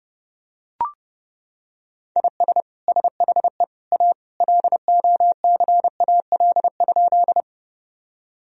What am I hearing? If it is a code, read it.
Morse code, heard as a keyed tone, about 29 wpm, 700 Hz. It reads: IS SHE A LOCAL?
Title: Sentences from Top 1000 Words - Review Part 3 - 30wpm